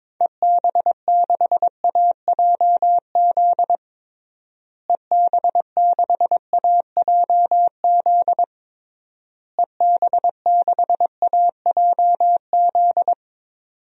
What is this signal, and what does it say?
Morse code, heard as a keyed tone, about 22 wpm, 700 Hz. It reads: EB6AJZ EB6AJZ EB6AJZ
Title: DXCC-100 Call Signs - 6 Characters - Repeated 3x - 22wpm